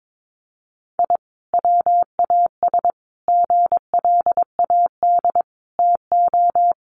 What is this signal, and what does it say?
Morse code, heard as a keyed tone, about 22 wpm, 700 Hz. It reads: I WAS GLAD TO
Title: Sentences from Top 1100 Words 22wpm